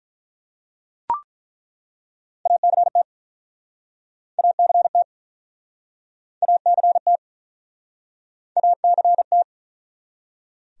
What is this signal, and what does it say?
Morse code, heard as a keyed tone, about 48 wpm, 700 Hz. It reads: ACT ACT ACT ACT